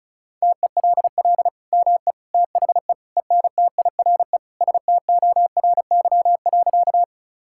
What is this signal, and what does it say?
Morse code, heard as a keyed tone, about 36 wpm, 700 Hz. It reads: TELL ME THE ENTIRE STORY.
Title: Sentences from Top 1000 Words - Review Part 5 - 35wpm